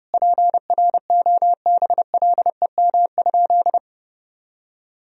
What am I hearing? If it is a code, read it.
Morse code, heard as a keyed tone, about 32 wpm, 700 Hz. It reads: PROBLEM?
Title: Sentences from Top 1000 Words - Review Part 3 - 30wpm